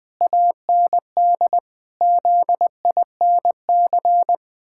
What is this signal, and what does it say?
Morse code, heard as a keyed tone, about 20 wpm, 700 Hz. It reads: AND ZINC
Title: Pangram Challenge v2 - 20wpm